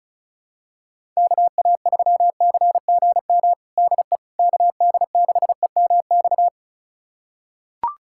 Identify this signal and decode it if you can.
Morse code, heard as a keyed tone, about 34 wpm, 700 Hz. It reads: KA3CGM DE KD6EMX
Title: USA Callsign DE Callsign 35wpm